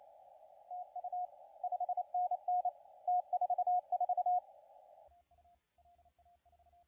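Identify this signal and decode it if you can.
Morse code, heard as a keyed tone, about 29 wpm, 720 Hz, with radio noise. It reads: TU 5NN T44
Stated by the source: CW contest audio recorded at LA4C during the SAC CW contest.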